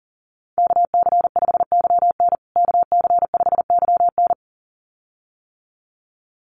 Morse code, heard as a keyed tone, about 37 wpm, 700 Hz. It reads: KC5YN KC5YN
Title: USA Call Signs Repeated 2x 40wpm